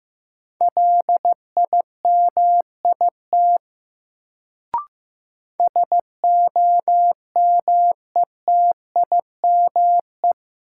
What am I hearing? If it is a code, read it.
Morse code, heard as a keyed tone, about 15 wpm, 700 Hz. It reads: LIMIT SOMETIME
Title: Sentences from Top 1500 Words - New Word Drill Alt - 15wpm